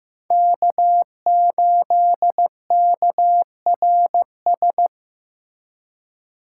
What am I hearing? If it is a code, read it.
Morse code, heard as a keyed tone, about 15 wpm, 700 Hz. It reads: K8KRS
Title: USA Call Signs - 5 Characters - 3x Character Spacing - 15wpm